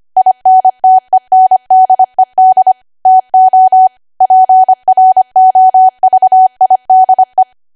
Morse code, heard as a keyed tone, about 25 wpm, 750 Hz. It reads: INTENDED TO PROVIDE
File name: Archive / 25 WPM / 25WPM